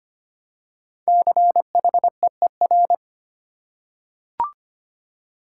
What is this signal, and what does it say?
Morse code, heard as a keyed tone, about 25 wpm, 700 Hz. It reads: CHEER
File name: 5 Letter Words - First 3 Letters Given - 25wpm